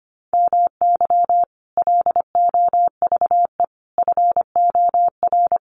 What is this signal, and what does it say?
Morse code, heard as a keyed tone, about 25 wpm, 700 Hz. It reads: MY LOVE FOR
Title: Sentences from Top 1400 Words 25wpm